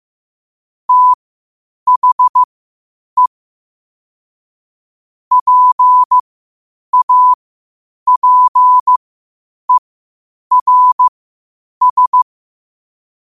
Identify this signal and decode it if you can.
Morse code, heard as a keyed tone, about 15 wpm, spaced as for 8 wpm, 1000 Hz. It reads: THE PAPERS
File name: qotdfiles / 8wpm